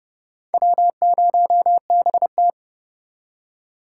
Morse code, heard as a keyed tone, about 40 wpm, spaced as for 35 wpm, 700 Hz. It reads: W0BT
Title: Speed Racing - USA 1x2 Call Signs 30wpm